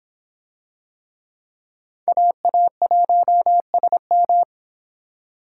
Morse code, heard as a keyed tone, about 26 wpm, 700 Hz. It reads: AA1SM